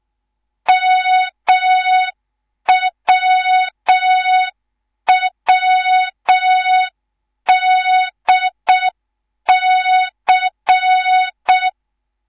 Morse code, heard as a keyed tone, about 6 wpm, 3700 Hz. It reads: MWWDC
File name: Morze-HA2MN / hang / 10GWZ HA2MN